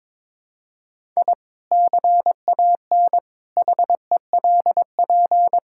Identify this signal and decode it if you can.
Morse code, heard as a keyed tone, about 22 wpm, 700 Hz. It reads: I CAN HELP
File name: practice / 22wpm / Sentences with 2-Word Phrases - 22wpm